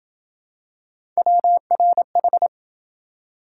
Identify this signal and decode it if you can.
Morse code, heard as a keyed tone, about 26 wpm, 700 Hz. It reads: WRH